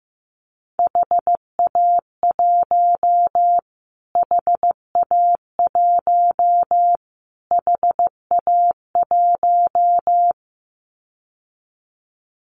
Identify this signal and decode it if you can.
Morse code, heard as a keyed tone, about 15 wpm, 700 Hz. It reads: HA1 HA1 HA1
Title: Callsign Prefixes for Top 25 DXCC - Repeated 3x - 15wpm